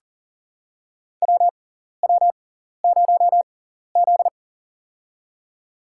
Morse code, heard as a keyed tone, about 39 wpm, spaced as for 14 wpm, 700 Hz. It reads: WW0Z
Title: USA Call Signs - 4 Characters - 6x Character Spacing - 40wpm